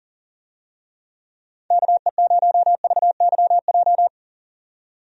Morse code, heard as a keyed tone, about 38 wpm, 700 Hz. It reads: KE0VYJ
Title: USA Call Signs - 6 Characters - 40wpm